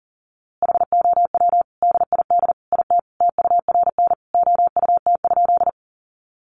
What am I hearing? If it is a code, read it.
Morse code, heard as a keyed tone, about 40 wpm, 700 Hz. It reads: HOW DID IT TURN OUT?
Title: A Sound Of Thunder 40wpm